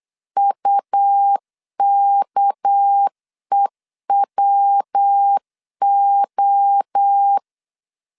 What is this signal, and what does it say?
Morse code, heard as a keyed tone, about 8 wpm, 800 Hz. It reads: UKEWO
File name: CW / DET / 1 6 1500C S B 8wpm 5142